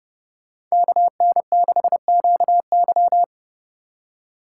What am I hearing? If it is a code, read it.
Morse code, heard as a keyed tone, about 37 wpm, 700 Hz. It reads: KN6QY
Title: Speed Racing - USA 2x2 Call Signs 30wpm